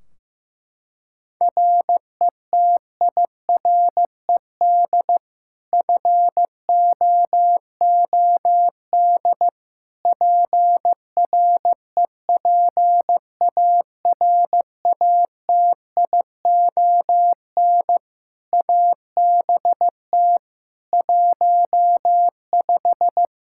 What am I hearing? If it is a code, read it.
Morse code, heard as a keyed tone, about 16 wpm, 700 Hz. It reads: RETIRED FOOD PREPARATION ABT 15
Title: K1USN SST QSO Practice 15wpm